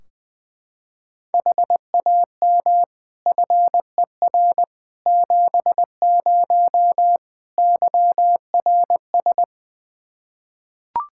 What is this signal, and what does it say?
Morse code, heard as a keyed tone, about 20 wpm, 700 Hz. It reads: HAM FER 70 YRS
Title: Practice QSO - Exchange 3 - More Info - 20wpm